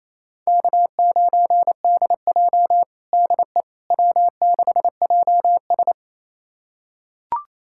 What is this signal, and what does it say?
Morse code, heard as a keyed tone, about 28 wpm, 700 Hz. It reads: K9DJ DE W6JS
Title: USA Callsign DE Callsign - 4 Characters - 28wpm